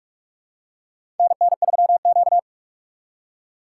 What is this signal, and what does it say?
Morse code, heard as a keyed tone, about 44 wpm, 690 Hz. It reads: NN3Q
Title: USA Call Signs - 4x Character Spacing - 45wpm